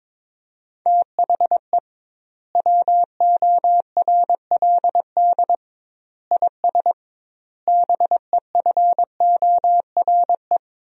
Morse code, heard as a keyed tone, about 22 wpm, 700 Hz. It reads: THE WORLD IS BEFORE
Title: 2x Word-Spacing - Sentences from Top 200 Words - 22wpm